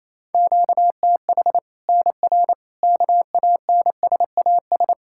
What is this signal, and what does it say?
Morse code, heard as a keyed tone, about 28 wpm, 700 Hz. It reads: QTH NR KANSAS
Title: Practice QSO - Exchange 1 - Beginning - 28wpm